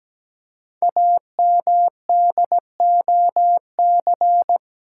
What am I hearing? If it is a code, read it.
Morse code, heard as a keyed tone, about 17 wpm, 700 Hz. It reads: AMDOC
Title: Code Groups - 5 Letters - 17wpm